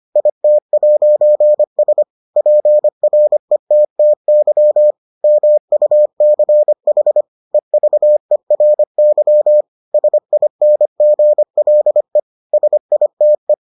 Morse code, heard as a keyed tone, about 25 wpm, 600 Hz. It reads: IT'S PRETTY MUCH EVERY SINGLE SITE